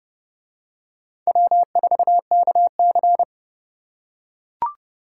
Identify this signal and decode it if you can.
Morse code, heard as a keyed tone, about 29 wpm, 700 Hz. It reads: W4KC